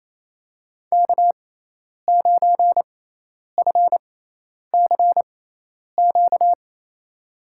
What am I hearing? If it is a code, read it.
Morse code, heard as a keyed tone, about 28 wpm, spaced as for 10 wpm, 700 Hz. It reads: K9FCQ
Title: USA Call Signs - 5 Characters - 6x Character Spacing - 28wpm